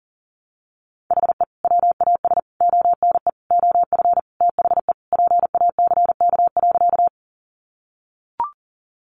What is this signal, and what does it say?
Morse code, heard as a keyed tone, about 40 wpm, 700 Hz. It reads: HE WAS ONE OF THE PACK.
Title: Sentences from Top 1300 Words 40wpm